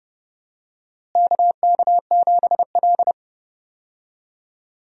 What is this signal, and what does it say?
Morse code, heard as a keyed tone, about 30 wpm, 700 Hz. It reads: KK7L
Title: USA Call Signs - 4 Characters - Repeated 3x - 30wpm